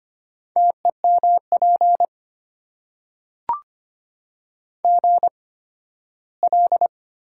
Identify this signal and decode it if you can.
Morse code, heard as a keyed tone, about 25 wpm, spaced as for 15 wpm, 700 Hz. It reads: TEMP GL
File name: Top 90 QSO Elements - 8x Character Spacing - 25wpm